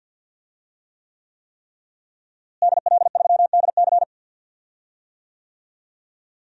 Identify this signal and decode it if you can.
Morse code, heard as a keyed tone, about 47 wpm, 700 Hz. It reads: DL3DC
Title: DXCC-100 Call Signs - 5 Characters - Repeated 3x - 50wpm